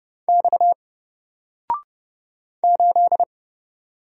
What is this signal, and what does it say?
Morse code, heard as a keyed tone, about 30 wpm, 700 Hz. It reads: X 8
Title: Single Character (Letter:Number:Punctuation:Prosign) 30wpm